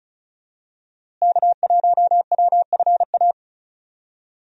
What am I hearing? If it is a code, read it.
Morse code, heard as a keyed tone, about 36 wpm, 700 Hz. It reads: K1WFA
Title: USA Call Signs - 5 Characters - Repeated 2x - 35wpm